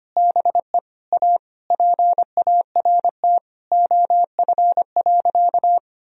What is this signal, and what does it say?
Morse code, heard as a keyed tone, about 25 wpm, 700 Hz. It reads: BE A PART OF.
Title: Sentences from Top 1000 Words - Review Part 1 - 25wpm